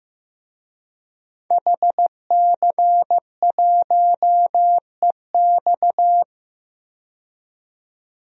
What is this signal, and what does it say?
Morse code, heard as a keyed tone, about 15 wpm, 700 Hz. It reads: HC1EX
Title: DXCC-100 Call Signs - 5 Characters - 15wpm